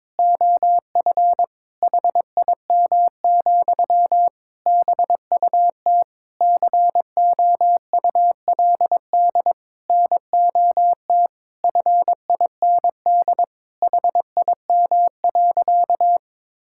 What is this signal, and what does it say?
Morse code, heard as a keyed tone, about 22 wpm, 700 Hz. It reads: OF HIM, BUT COULD NOT FIND HIM.
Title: Sentences from Top 1000 Words - Review Part 5 - 22wpm